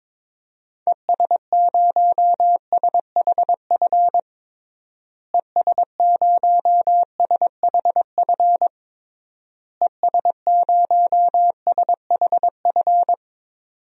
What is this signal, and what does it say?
Morse code, heard as a keyed tone, about 22 wpm, 700 Hz. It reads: ES0SHF ES0SHF ES0SHF
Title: DXCC-100 Call Signs - 6 Characters - Repeated 3x - 22wpm